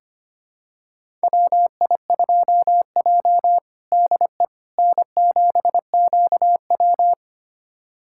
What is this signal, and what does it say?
Morse code, heard as a keyed tone, about 25 wpm, 700 Hz. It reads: WI2J DE N7QW
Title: USA Callsign DE Callsign - 4 Characters - 25wpm